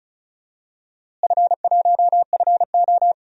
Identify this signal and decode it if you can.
Morse code, heard as a keyed tone, about 36 wpm, 690 Hz. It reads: F1FO
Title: DXCC-100 Call Signs - 4 Characters - Repeated 3x - 35wpm